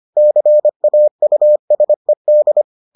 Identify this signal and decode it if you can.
Morse code, heard as a keyed tone, about 25 wpm, 600 Hz. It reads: CAUSED